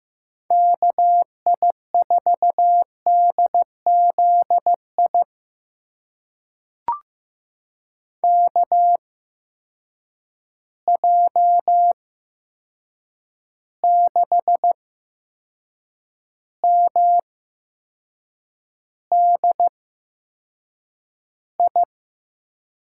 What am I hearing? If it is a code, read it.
Morse code, heard as a keyed tone, about 15 wpm, spaced as for 6 wpm, 700 Hz. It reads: KI4DZI KJ6MDI
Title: USA Call Signs - 6 Characters - 8x Character Spacing - 15wpm